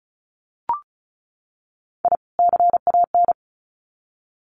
Morse code, heard as a keyed tone, about 34 wpm, 700 Hz. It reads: I CAN
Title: Sets of 2 Words 35wpm